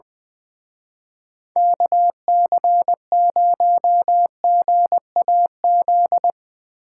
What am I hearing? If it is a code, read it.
Morse code, heard as a keyed tone, about 23 wpm, 700 Hz. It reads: KC0GAZ
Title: Speed Racing - USA 2x3 Call Signs 20wpm